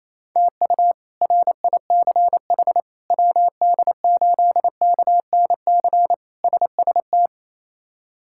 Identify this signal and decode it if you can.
Morse code, heard as a keyed tone, about 28 wpm, 700 Hz. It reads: TU RICH WD8KNC SST
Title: K1USN SST QSO Practice - 28wpm